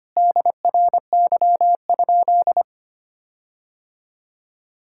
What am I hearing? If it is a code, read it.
Morse code, heard as a keyed tone, about 25 wpm, 700 Hz. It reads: DRY?
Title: Sentences from Top 1600 Words 25wpm